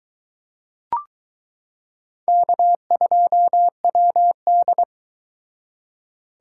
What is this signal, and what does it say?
Morse code, heard as a keyed tone, about 22 wpm, 700 Hz. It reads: K2WD